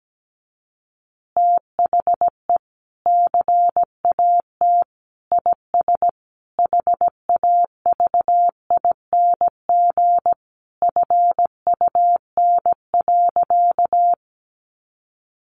Morse code, heard as a keyed tone, about 17 wpm, 700 Hz. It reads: THE CAT IS HAVING FUN.
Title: Easy Phrases 17wpm